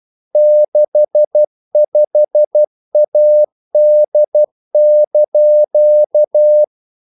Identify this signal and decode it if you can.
Morse code, heard as a keyed tone, about 12 wpm, 600 Hz. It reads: 65AD)
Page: English quotations in CW (telegraphy - morse)